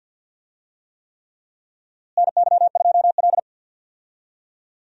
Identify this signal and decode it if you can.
Morse code, heard as a keyed tone, about 52 wpm, 700 Hz. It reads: NY2L
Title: USA Call Signs - 4 Characters - Repeated 3x - 50wpm